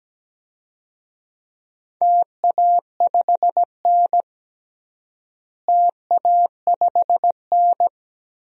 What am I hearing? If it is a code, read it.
Morse code, heard as a keyed tone, about 17 wpm, 700 Hz. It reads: TA5N TA5N
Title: DXCC-100 Call Signs - 4 Characters - Repeated 3x - 17wpm